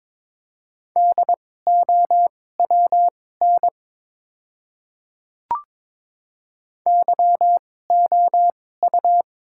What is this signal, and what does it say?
Morse code, heard as a keyed tone, about 22 wpm, spaced as for 16 wpm, 700 Hz. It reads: DOWN YOU
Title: 2x Character Spacing - Top 100 Words - 22wpm